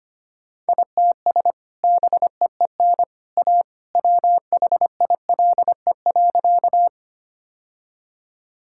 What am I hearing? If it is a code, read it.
Morse code, heard as a keyed tone, about 25 wpm, 700 Hz. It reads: ITS BEEN A WHILE.
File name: practice / 25wpm / Sentences with 2-Word Phrases - 25wpm